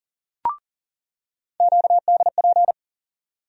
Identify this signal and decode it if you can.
Morse code, heard as a keyed tone, about 40 wpm, 700 Hz. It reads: QDP
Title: USA Call Sign Postfixes - 3 Letters 40wpm